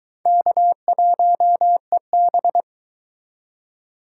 Morse code, heard as a keyed tone, about 22 wpm, 700 Hz. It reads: K1EB